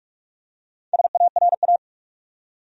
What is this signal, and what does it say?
Morse code, heard as a keyed tone, about 45 wpm, 700 Hz. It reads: SARA
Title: USA Common Names - Max 5 Letters - 45wpm